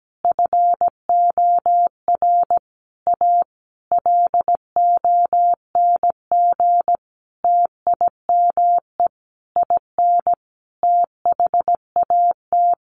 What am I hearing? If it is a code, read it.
Morse code, heard as a keyed tone, about 17 wpm, 700 Hz. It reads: FOR A LONG TIME IN THAT
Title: Sentences from Top 900 Words 17wpm